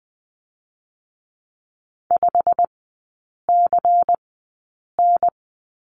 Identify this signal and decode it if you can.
Morse code, heard as a keyed tone, about 20 wpm, spaced as for 8 wpm, 700 Hz. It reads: 5CN
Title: 093 Lesson 16e - Letter c - Three Characters 20wpm